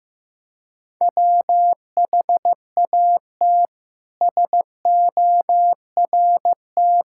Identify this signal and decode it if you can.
Morse code, heard as a keyed tone, about 15 wpm, 700 Hz. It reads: WHAT SORT
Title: Sentences from Top 1600 Words 15wpm